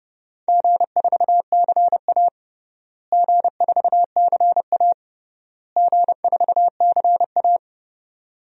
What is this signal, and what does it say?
Morse code, heard as a keyed tone, about 30 wpm, 700 Hz. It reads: G4CA G4CA G4CA
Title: DXCC-100 Call Signs - 4 Characters - Repeated 3x - 30wpm